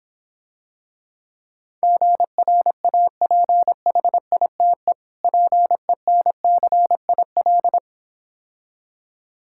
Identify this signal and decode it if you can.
Morse code, heard as a keyed tone, about 26 wpm, 700 Hz. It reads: GRAPHITE PENCIL